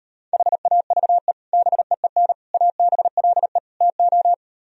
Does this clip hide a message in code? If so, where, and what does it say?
Yes: Morse code from 0.3 to 4.3 s, heard as a keyed tone, about 38 wpm, 700 Hz. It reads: HAVE BEEN ABLE TO